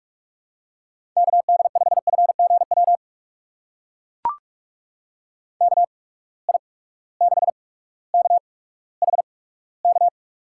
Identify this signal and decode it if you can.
Morse code, heard as a keyed tone, about 44 wpm, spaced as for 19 wpm, 700 Hz. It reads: KD5FGW KI6KHK
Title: USA Call Signs - 6 Characters - 8x Character Spacing - 45wpm